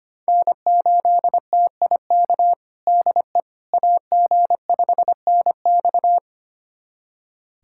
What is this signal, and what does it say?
Morse code, heard as a keyed tone, about 25 wpm, 700 Hz. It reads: N8TIK DE AG5NX
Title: USA Callsign DE Callsign - 5 Characters - 25wpm